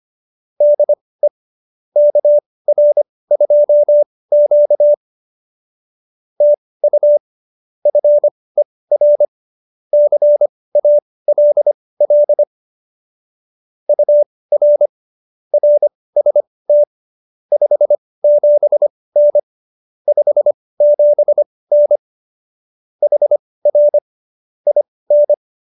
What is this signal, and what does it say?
Morse code, heard as a keyed tone, about 25 wpm, spaced as for 18 wpm, 590 Hz. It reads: DE KR2Q TU FER CALL UR RST 57N 57N HR IN